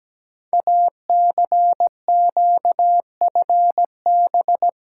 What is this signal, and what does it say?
Morse code, heard as a keyed tone, about 17 wpm, 700 Hz. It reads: ACQFB
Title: Code Groups - 5 Letters - 17wpm